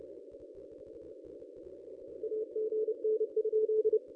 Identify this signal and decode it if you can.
Morse code, heard as a keyed tone, about 30 wpm, 430 Hz. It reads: AGN?
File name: contests / ss-cw-k5tr-op-n5rz -1000